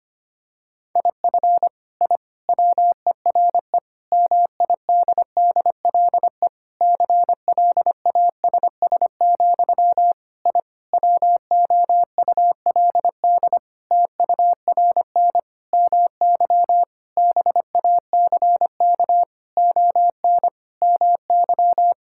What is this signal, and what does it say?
Morse code, heard as a keyed tone, about 25 wpm, 700 Hz. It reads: IF I WERE MIDDLE CLASS, I WOULD TURN MY BACK ON MY
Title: Sentences from Top 1000 Words - Review Part 6 - 25wpm